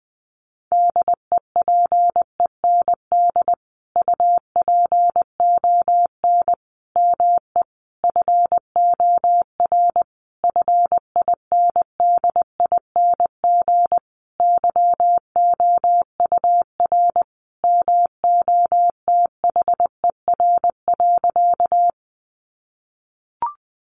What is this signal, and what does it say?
Morse code, heard as a keyed tone, about 20 wpm, 700 Hz. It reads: DEPEND UPON ME FOR FINDING YOUR MOTHER.
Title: Sentences from Top 1300 Words 20wpm